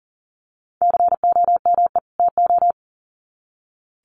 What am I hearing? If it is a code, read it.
Morse code, heard as a keyed tone, about 40 wpm, 700 Hz. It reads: COME TO
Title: Sets of 2 Words 40wpm